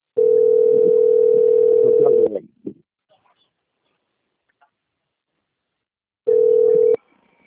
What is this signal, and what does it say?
Ringback tone (North America), 440 + 480 Hz.
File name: out- - - - - .27987